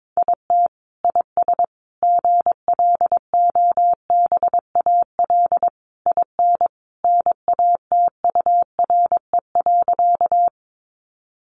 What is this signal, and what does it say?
Morse code, heard as a keyed tone, about 22 wpm, 700 Hz. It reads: IT IS GLOBAL IN NATURE.
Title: Sentences from Top 1200 Words 22wpm